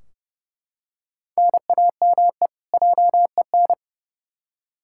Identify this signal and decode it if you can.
Morse code, heard as a keyed tone, about 32 wpm, 700 Hz. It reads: NAME JEN
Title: K1USN SST QSO Practice 30wpm